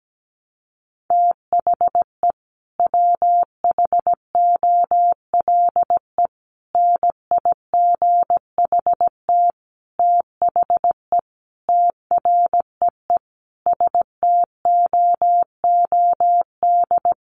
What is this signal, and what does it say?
Morse code, heard as a keyed tone, about 17 wpm, 700 Hz. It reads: THE WHOLE NIGHT THE TREE STOOD
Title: Sentences from Top 500 Words 17wpm